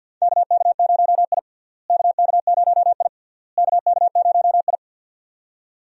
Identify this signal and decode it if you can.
Morse code, heard as a keyed tone, about 48 wpm, 700 Hz. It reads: KK0I KK0I KK0I
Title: USA Call Signs - 4 Characters - Repeated 3x - 50wpm